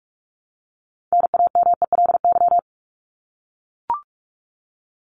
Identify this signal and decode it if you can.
Morse code, heard as a keyed tone, about 44 wpm, 700 Hz. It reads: NAMELY
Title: LY Suffix - Top 100 Words with Base Words 45wpm